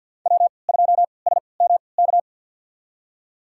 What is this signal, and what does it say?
Morse code, heard as a keyed tone, about 47 wpm, spaced as for 27 wpm, 700 Hz. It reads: W2SMK
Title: USA Call Signs - 5 Characters - 3x Character Spacing - 50wpm